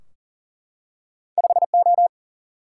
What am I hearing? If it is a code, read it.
Morse code, heard as a keyed tone, about 40 wpm, 700 Hz. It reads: 5O
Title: Single-Letter Single-Number 40wpm